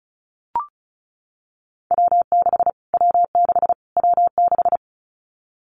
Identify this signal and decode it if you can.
Morse code, heard as a keyed tone, about 34 wpm, 700 Hz. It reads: W6 W6 W6
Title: Callsign Prefixes for Top 25 DXCC - Repeated 3x - 35wpm